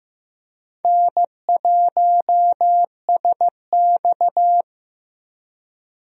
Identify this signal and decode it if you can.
Morse code, heard as a keyed tone, about 15 wpm, 700 Hz. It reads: N1SX
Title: USA Call Signs - 4 Characters - Repeated 3x - 15wpm